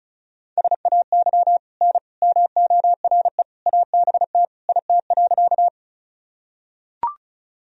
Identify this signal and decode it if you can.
Morse code, heard as a keyed tone, about 34 wpm, 700 Hz. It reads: SAY N MORE ABT IT.
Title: Sentences from Top 200 Words and Abbreviations 35wpm